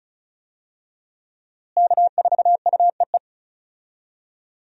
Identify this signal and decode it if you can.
Morse code, heard as a keyed tone, about 36 wpm, 700 Hz. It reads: K4UEE
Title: USA Call Signs - 5 Characters - Repeated 3x - 35wpm